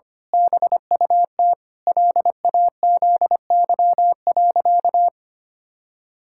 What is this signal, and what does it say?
Morse code, heard as a keyed tone, about 25 wpm, 700 Hz. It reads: BUT LAZY.
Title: Pangram Challenge v2 - 25wpm